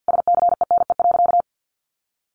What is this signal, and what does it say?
Morse code, heard as a keyed tone, about 48 wpm, 700 Hz. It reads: SCENE.
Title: Sentences from Top 1100 Words 50wpm